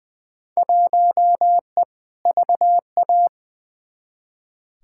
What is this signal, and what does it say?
Morse code, heard as a keyed tone, about 20 wpm, 700 Hz. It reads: 1E VA
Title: Field Day Exchanges - 20wpm